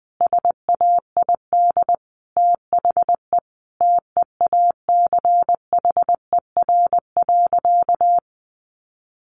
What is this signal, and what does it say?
Morse code, heard as a keyed tone, about 20 wpm, 700 Hz. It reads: SAID THE TEACHER.
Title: Sentences from Top 900 Words 20wpm